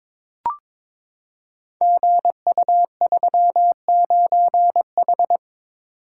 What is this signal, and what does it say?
Morse code, heard as a keyed tone, about 22 wpm, 700 Hz. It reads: GU39H
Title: Code Groups - 5 Letter:Numbers - 22wpm